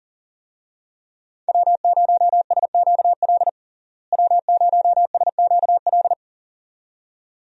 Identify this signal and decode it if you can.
Morse code, heard as a keyed tone, about 40 wpm, 700 Hz. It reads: W0SQL W0SQL
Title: USA Call Signs - 5 Characters - Repeated 2x - 40wpm